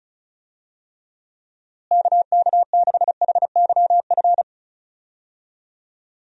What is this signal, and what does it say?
Morse code, heard as a keyed tone, about 34 wpm, 700 Hz. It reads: KK6HYF
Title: USA Call Signs - 6 Characters - Repeated 3x - 35wpm